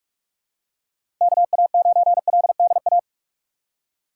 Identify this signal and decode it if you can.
Morse code, heard as a keyed tone, about 46 wpm, 700 Hz. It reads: KA9LDA
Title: USA Call Signs - 6 Characters - 3x Character Spacing - 45wpm